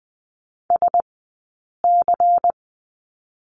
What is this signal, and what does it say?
Morse code, heard as a keyed tone, about 20 wpm, 700 Hz. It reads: S C
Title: 099 Lesson 17d - Letter u - Two Characters 20wpm